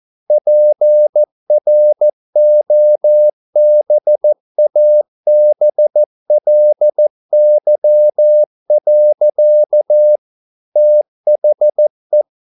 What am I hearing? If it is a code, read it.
Morse code, heard as a keyed tone, about 14 wpm, 600 Hz. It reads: PROBABLY. THE